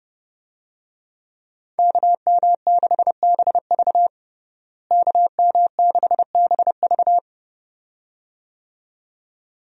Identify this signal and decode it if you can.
Morse code, heard as a keyed tone, about 31 wpm, 700 Hz. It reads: KM6BV KM6BV
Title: USA Call Signs - 5 Characters - Repeated 2x - 30wpm